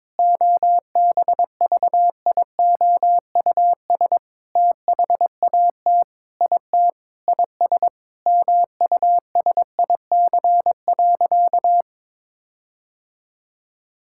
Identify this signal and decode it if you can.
Morse code, heard as a keyed tone, about 22 wpm, 700 Hz. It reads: OBVIOUS THAT IT IS MUSIC.
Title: Sentences from Top 1600 Words 22wpm